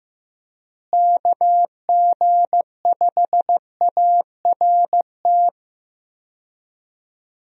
Morse code, heard as a keyed tone, about 15 wpm, 700 Hz. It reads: KG5ART
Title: USA Call Signs - 6 Characters - Repeated 2x - 15wpm